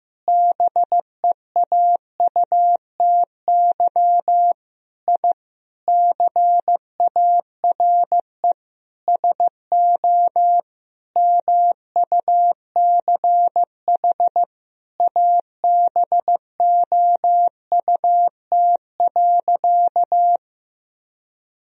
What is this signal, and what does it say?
Morse code, heard as a keyed tone, about 15 wpm, 700 Hz. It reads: BEAUTY I CARE SO MUCH ABOUT.
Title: Sentences from Top 1000 Words - Review Part 2 - 15wpm